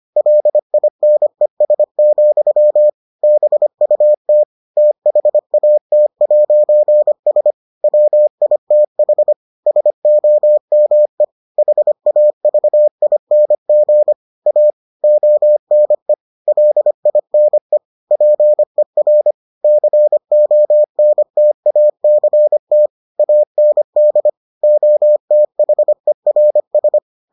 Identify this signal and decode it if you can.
Morse code, heard as a keyed tone, about 25 wpm, 600 Hz. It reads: LINES, BUT THAT'S WITH SOME HAVING A ONE LINE PER CONTACT AND OTHERS